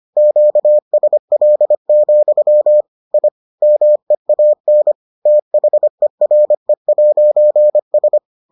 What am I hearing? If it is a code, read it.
Morse code, heard as a keyed tone, about 25 wpm, 600 Hz. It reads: QSL, I MEAN THERE'S